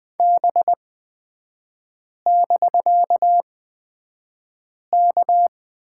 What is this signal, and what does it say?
Morse code, heard as a keyed tone, about 20 wpm, spaced as for 5 wpm, 700 Hz. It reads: B<BK>K
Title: 250 Lesson 40e - BK - Three Characters 20wpm